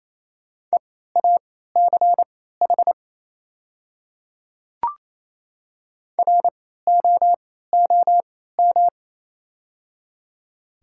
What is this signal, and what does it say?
Morse code, heard as a keyed tone, about 28 wpm, spaced as for 16 wpm, 700 Hz. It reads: EACH ROOM
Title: 3x Character Spacing - Top 300 Words - 28wpm